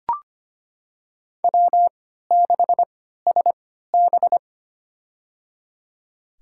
Morse code, heard as a keyed tone, about 25 wpm, spaced as for 14 wpm, 700 Hz. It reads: W6SB
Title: USA Call Signs - 3x Character Spacing - 25wpm